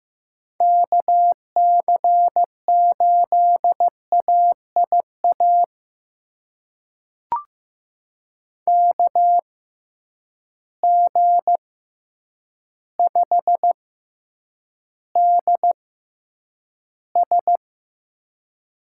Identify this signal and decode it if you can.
Morse code, heard as a keyed tone, about 15 wpm, spaced as for 8 wpm, 700 Hz. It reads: KC8AIA KG5DS
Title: USA Call Signs - 6 Characters - 6x Character Spacing - 15wpm